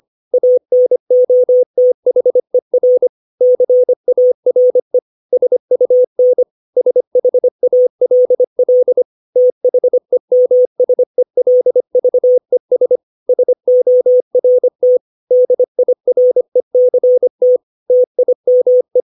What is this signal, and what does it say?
Morse code, heard as a keyed tone, about 25 wpm, 500 Hz. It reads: ANOTHER CARE SUN SHALL THEMSELVES SORT DIRECT TIME
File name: 632 words / 632 Words 25WPM